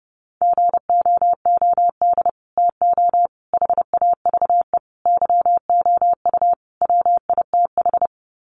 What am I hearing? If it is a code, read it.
Morse code, heard as a keyed tone, about 30 wpm, 700 Hz. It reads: GOOD TO HAVE YOU WITH
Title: Sets of 5 Words Encore2 30wpm